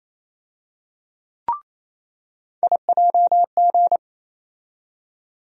Morse code, heard as a keyed tone, about 28 wpm, 700 Hz. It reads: IJG
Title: USA Call Sign Postfixes - 3 Letters 28wpm